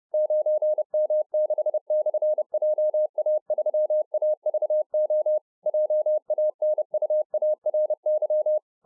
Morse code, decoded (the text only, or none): 9M6/JA3AVO JANUARY